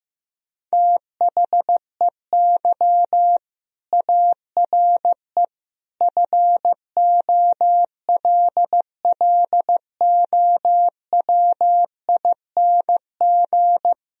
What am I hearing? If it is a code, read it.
Morse code, heard as a keyed tone, about 15 wpm, 700 Hz. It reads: THEY ARE FOLLOWING
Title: Sentences from Top 1000 Words - Review Part 4 - 15wpm